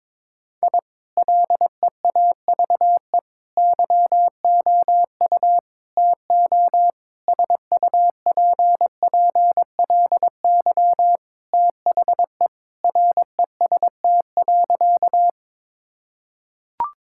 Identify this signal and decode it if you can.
Morse code, heard as a keyed tone, about 22 wpm, 700 Hz. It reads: I LEAVE YOU TO SUPPLY THE REST.
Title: Sentences from Top 1000 Words - Review Part 5 - 22wpm